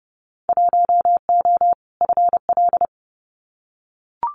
Morse code, heard as a keyed tone, about 30 wpm, 700 Hz. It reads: 1O FL
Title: Winter Field Day Exchanges 30wpm